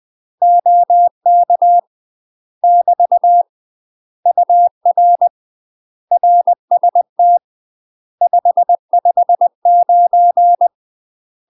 Morse code, heard as a keyed tone, about 20 wpm, 700 Hz. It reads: OK <BT> UR RST 559
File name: UK MASTER QSOs TRAINING / 20WPM / UK MASTER QSOs 0058